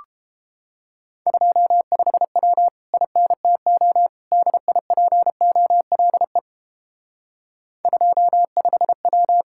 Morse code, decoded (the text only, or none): 25W INTO DIPOLE 25W